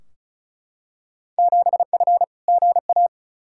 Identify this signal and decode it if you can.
Morse code, heard as a keyed tone, about 34 wpm, 700 Hz. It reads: 7F GA
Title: Field Day Exchanges Encore - 35wpm